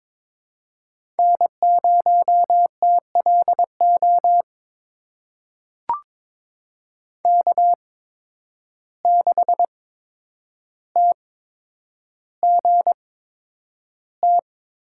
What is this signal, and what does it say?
Morse code, heard as a keyed tone, about 22 wpm, spaced as for 9 wpm, 700 Hz. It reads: N0TLO K6TGT
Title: USA Call Signs - 5 Characters - 8x Character Spacing - 22wpm